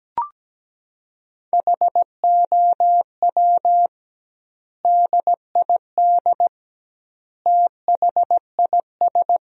Morse code, heard as a keyed tone, about 17 wpm, 700 Hz. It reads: HOW DID THIS
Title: 2x Word-Spacing - Sentences from Top 400 Words - 17wpm